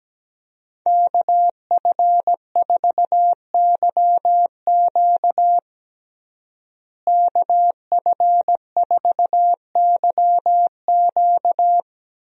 Morse code, heard as a keyed tone, about 17 wpm, 700 Hz. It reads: KF4YQ KF4YQ
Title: USA Call Signs - 5 Characters - Repeated 2x - 17wpm